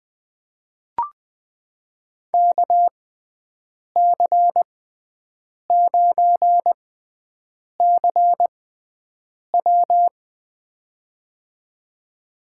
Morse code, heard as a keyed tone, about 20 wpm, spaced as for 7 wpm, 700 Hz. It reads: KC9CW
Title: USA Call Signs - 5 Characters - 6x Character Spacing - 20wpm